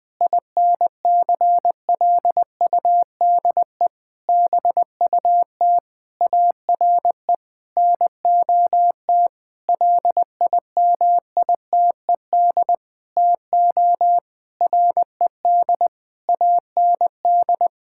Morse code, heard as a keyed tone, about 20 wpm, 700 Hz. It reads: INCLUDE BUT ARE NOT LIMITED TO RED AND
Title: Sentences with 6-Word Phrases - Spoken Beginning - 20wpm